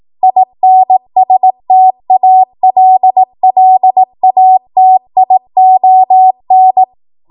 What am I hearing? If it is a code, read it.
Morse code, heard as a keyed tone, about 18 wpm, 750 Hz. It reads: INSTALLATION
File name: Morse / 18WPM